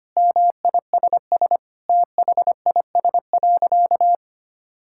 Morse code, heard as a keyed tone, about 25 wpm, 700 Hz. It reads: MISS THIS.
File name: practice / 25wpm / Sentences with 4-Word Phrases - 25wpm